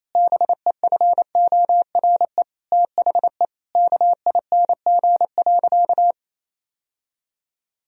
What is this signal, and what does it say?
Morse code, heard as a keyed tone, about 28 wpm, 700 Hz. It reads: BEFORE THE KING.
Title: Sentences from Top 1000 Words - Review Part 2 - 28wpm